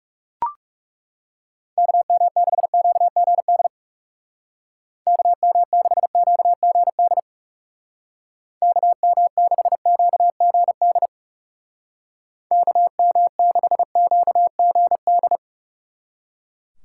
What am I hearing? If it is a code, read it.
Morse code, heard as a keyed tone, about 37 wpm, 700 Hz. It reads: KM6QGD KM6QGD KM6QGD KM6QGD
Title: Speed Racing - USA 2x3 Call Signs 30wpm